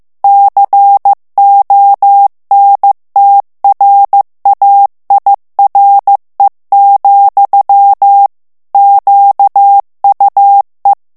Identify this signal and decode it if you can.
Morse code, heard as a keyed tone, about 15 wpm, 800 Hz. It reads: CONTRAIRE, QUE
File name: morse mp3 / FRANCAIS TEXTE CLAIR 2 VITESSE 14 MOTS MINUTES